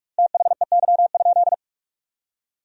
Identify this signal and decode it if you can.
Morse code, heard as a keyed tone, about 41 wpm, 700 Hz. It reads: THEY?
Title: 2x Word-Spacing - Sentences from Top 300 Words - 45wpm